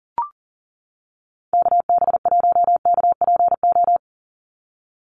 Morse code, heard as a keyed tone, about 40 wpm, 700 Hz. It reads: KB1KPO
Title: USA Call Signs 40wpm